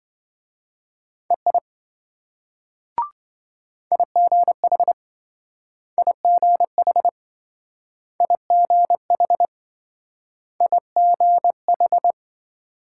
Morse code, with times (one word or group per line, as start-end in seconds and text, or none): 1.3-1.6 EI
3.9-4.9 IGH
6.0-7.1 IGH
8.2-9.5 IGH
10.6-12.1 IGH